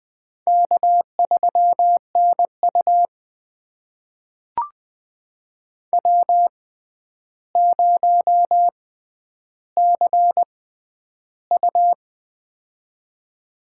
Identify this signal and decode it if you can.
Morse code, heard as a keyed tone, about 20 wpm, spaced as for 10 wpm, 700 Hz. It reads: K3NU W0CU
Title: USA Call Signs - 4 Characters - 6x Character Spacing - 20wpm